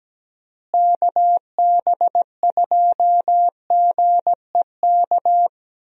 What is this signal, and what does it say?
Morse code, heard as a keyed tone, about 17 wpm, 700 Hz. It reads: KB2GEK
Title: USA Call Signs - 6 Characters - Repeated 2x - 17wpm